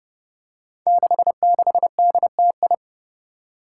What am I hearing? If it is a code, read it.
Morse code, heard as a keyed tone, about 28 wpm, 700 Hz. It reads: 66DTI
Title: Five Challenging Letter-Number Combinations 30wpm